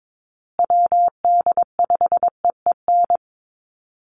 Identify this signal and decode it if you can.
Morse code, heard as a keyed tone, about 22 wpm, 700 Hz. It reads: WD5EEN
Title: USA Call Signs Repeated 3x 22wpm